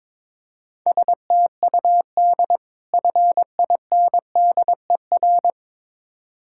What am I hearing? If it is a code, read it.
Morse code, heard as a keyed tone, about 22 wpm, 690 Hz. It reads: STUD FINDER
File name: Difficult Words with Hints - 22wpm